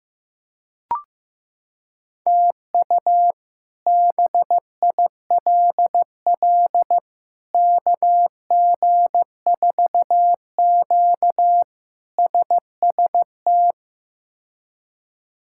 Morse code, heard as a keyed tone, about 15 wpm, 700 Hz. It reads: TU BILL KG4Q SST
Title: K1USN SST QSO Practice - 15wpm